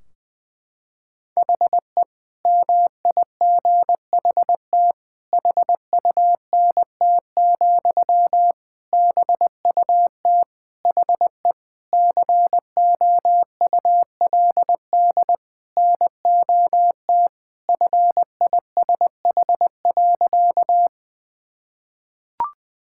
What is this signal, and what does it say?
Morse code, heard as a keyed tone, about 20 wpm, 700 Hz. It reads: HE MIGHT HUNT, BUT HE COULD NOT FISH.
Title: Sentences from Top 1600 Words 20wpm